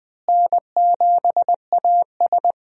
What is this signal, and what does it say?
Morse code, heard as a keyed tone, about 21 wpm, 700 Hz. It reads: N7AS
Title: USA Call Signs - 4 Characters - Repeated 3x - 20wpm